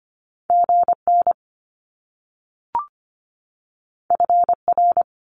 Morse code, heard as a keyed tone, about 25 wpm, 700 Hz. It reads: GN FR
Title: Top Two Letter Combinations in Words 25wpm